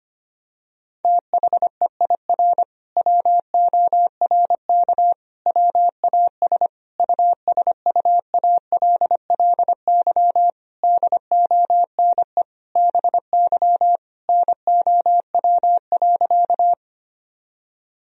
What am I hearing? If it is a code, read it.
Morse code, heard as a keyed tone, about 25 wpm, 700 Hz. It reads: THEIR WORK WAS USUALLY DONE BY NOW.
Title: Sentences from Top 1000 Words - Review Part 3 - 25wpm